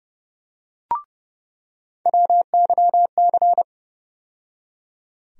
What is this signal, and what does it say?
Morse code, heard as a keyed tone, about 30 wpm, 700 Hz. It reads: WYC
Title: USA Call Sign Postfixes - 3 Letters 30wpm